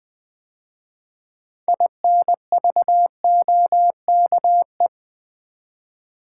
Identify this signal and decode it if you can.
Morse code, heard as a keyed tone, about 20 wpm, 700 Hz. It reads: INVOKE
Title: 6 Letter Words - First 3 Letters Given - 20wpm